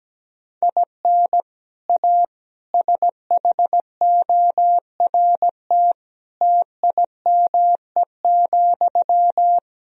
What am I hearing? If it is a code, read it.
Morse code, heard as a keyed tone, about 17 wpm, 700 Hz. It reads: IN A SHORT TIME,
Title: Sentences from Top 1000 Words - Review Part 2 - 17wpm